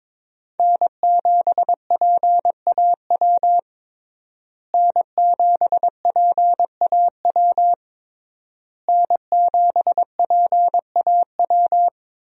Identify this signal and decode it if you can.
Morse code, heard as a keyed tone, about 21 wpm, 700 Hz. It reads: N7PAW N7PAW N7PAW
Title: USA Call Signs - 5 Characters - Repeated 3x - 22wpm